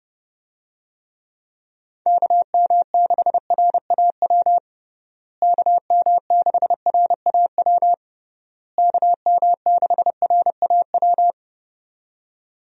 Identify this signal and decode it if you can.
Morse code, heard as a keyed tone, about 32 wpm, 700 Hz. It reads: KM6RAW KM6RAW KM6RAW
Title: USA Call Signs - 6 Characters - Repeated 3x - 30wpm